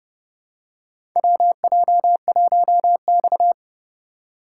Morse code, heard as a keyed tone, about 30 wpm, 700 Hz. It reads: WJ1X
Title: USA Call Signs - 4 Characters - 2x Character Spacing - 30wpm